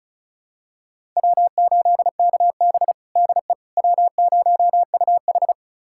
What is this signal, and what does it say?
Morse code, heard as a keyed tone, about 34 wpm, 700 Hz. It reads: W8KB DE W0UH
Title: USA Callsign DE Callsign - 4 Characters - 35wpm